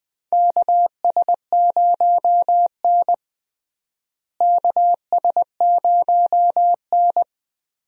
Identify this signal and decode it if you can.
Morse code, heard as a keyed tone, about 20 wpm, 700 Hz. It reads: KS0N KS0N
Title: USA Call Signs - 4 Characters - Repeated 2x - 20wpm